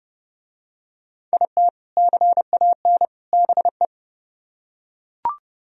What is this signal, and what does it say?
Morse code, heard as a keyed tone, about 30 wpm, 700 Hz. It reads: IT CAN BE